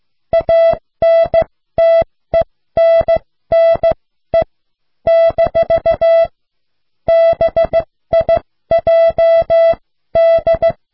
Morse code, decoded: ANTENNE - BIJD